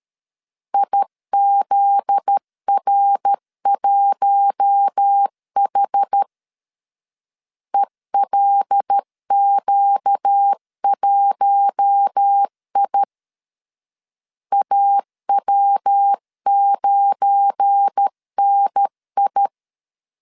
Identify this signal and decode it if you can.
Morse code, heard as a keyed tone, about 13 wpm, 800 Hz. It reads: IZR1H ELQ1I AW9NI
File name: LEZIONI / 7 GF9Z / 7 GF9Z 12wpm 2147